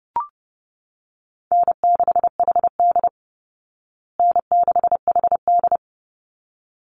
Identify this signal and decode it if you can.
Morse code, heard as a keyed tone, about 28 wpm, 700 Hz. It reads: N6HD N6HD
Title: Easy Call Signs Repeated 2x - 30wpm